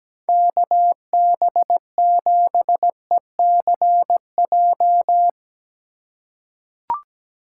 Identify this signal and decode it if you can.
Morse code, heard as a keyed tone, about 17 wpm, 700 Hz. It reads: KB7ECJ
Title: USA Call Signs - 6 Characters - 3x Character Spacing - 17wpm